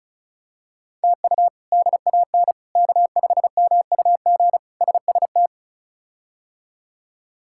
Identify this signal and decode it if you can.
Morse code, heard as a keyed tone, about 34 wpm, 700 Hz. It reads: TU DAN K5MUG SST
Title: K1USN SST QSO Practice - 35wpm